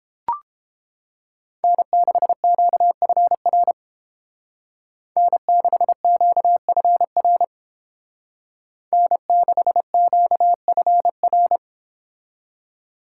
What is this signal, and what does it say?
Morse code, heard as a keyed tone, about 30 wpm, 700 Hz. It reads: N6QFR N6QFR N6QFR